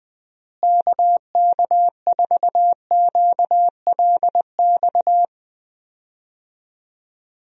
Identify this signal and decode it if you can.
Morse code, heard as a keyed tone, about 21 wpm, 700 Hz. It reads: KK4QLX
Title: USA Call Signs - 6 Characters - Repeated 3x - 20wpm